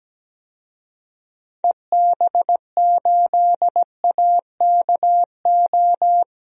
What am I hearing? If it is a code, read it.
Morse code, heard as a keyed tone, about 17 wpm, 700 Hz. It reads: EB8AKO
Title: DXCC-100 Call Signs - 6 Characters - Repeated 3x - 17wpm